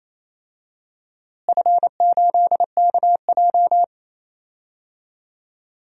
Morse code, heard as a keyed tone, about 28 wpm, 700 Hz. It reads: F8KJ
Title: DXCC-100 Call Signs - 4 Characters - Repeated 2x - 28wpm